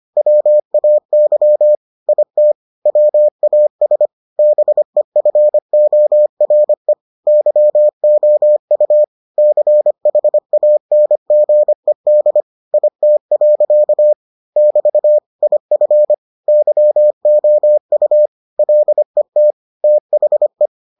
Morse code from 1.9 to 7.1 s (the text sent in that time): IT WAS BEFORE